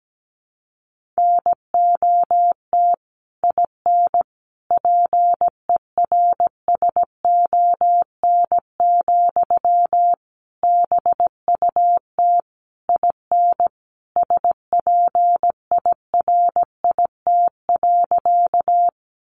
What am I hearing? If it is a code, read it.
Morse code, heard as a keyed tone, about 17 wpm, 700 Hz. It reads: NOT IN PERSON, BUT IN SPIRIT.
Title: Sentences from Top 1400 Words 17wpm